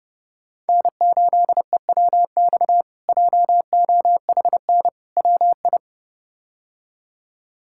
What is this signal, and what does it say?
Morse code, heard as a keyed tone, about 30 wpm, 700 Hz. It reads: N8EWX JOHN WI
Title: K1USN SST QSO Practice - 30wpm